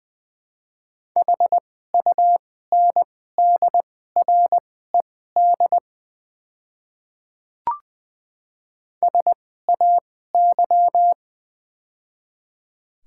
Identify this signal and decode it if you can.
Morse code, heard as a keyed tone, about 21 wpm, spaced as for 15 wpm, 690 Hz. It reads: HUNDRED SAY
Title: 2x Character Spacing - Top 500 Words - 20wpm